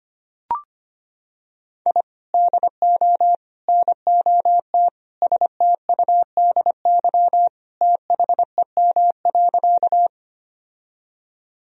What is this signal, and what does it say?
Morse code, heard as a keyed tone, about 25 wpm, 700 Hz. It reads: I DO NOT STUDY THEM.
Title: Sentences from Top 1000 Words - Review Part 1 - 25wpm